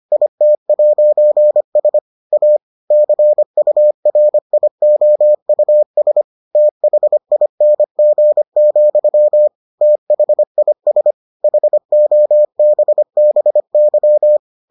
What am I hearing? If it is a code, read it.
Morse code, heard as a keyed tone, about 25 wpm, 600 Hz. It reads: IT'S A CURIOUS THING, THIS HOBBY